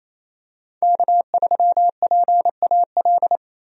Morse code, heard as a keyed tone, about 28 wpm, 700 Hz. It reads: K3PAL
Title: USA Call Signs - 5 Characters - Repeated 2x - 28wpm